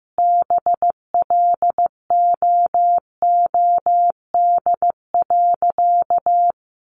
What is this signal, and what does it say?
Morse code, heard as a keyed tone, about 15 wpm, 700 Hz. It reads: BLOOD.
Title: Sentences from Top 1100 Words 15wpm